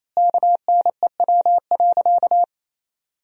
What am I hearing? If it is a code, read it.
Morse code, heard as a keyed tone, about 27 wpm, 700 Hz. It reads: KNEW.
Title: 4x Word-Spacing - Sentences from Top 400 Words - 28wpm